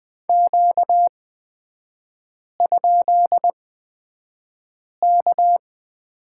Morse code, heard as a keyed tone, about 20 wpm, spaced as for 5 wpm, 700 Hz. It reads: Q?K
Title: 243 Lesson 39e - Letter z - Three Characters 20wpm